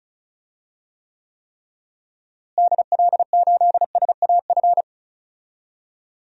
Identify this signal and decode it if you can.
Morse code, heard as a keyed tone, about 34 wpm, 700 Hz. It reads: DL8SAF
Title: DXCC-100 Call Signs - 6 Characters - Repeated 3x - 35wpm